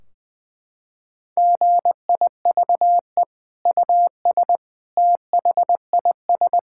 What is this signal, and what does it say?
Morse code, heard as a keyed tone, about 20 wpm, 700 Hz. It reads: GIVE US THIS
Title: Sentences from Top 1600 Words 20wpm